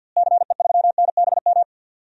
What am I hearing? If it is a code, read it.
Morse code, heard as a keyed tone, about 50 wpm, 700 Hz. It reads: CE3NBM
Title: DXCC-100 Call Signs - 6 Characters - Repeated 3x - 50wpm